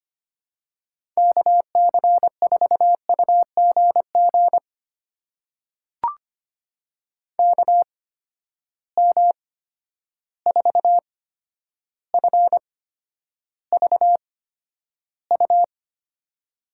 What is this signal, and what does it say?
Morse code, heard as a keyed tone, about 25 wpm, spaced as for 11 wpm, 700 Hz. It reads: KC4UGG KM4FVU
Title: USA Call Signs - 6 Characters - 8x Character Spacing - 25wpm